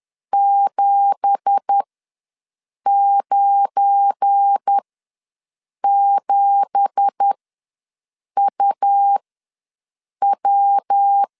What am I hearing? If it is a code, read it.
Morse code, heard as a keyed tone, about 10 wpm, spaced as for 6 wpm, 800 Hz. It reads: 797UW
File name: LEZIONI / 9 2867 ws 10wpm 4215